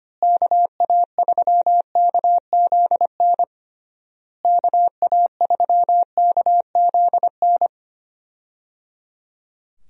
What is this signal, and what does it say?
Morse code, heard as a keyed tone, about 25 wpm, 700 Hz. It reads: KA3KZN KA3KZN
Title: USA Call Signs - 6 Characters - Repeated 3x - 25wpm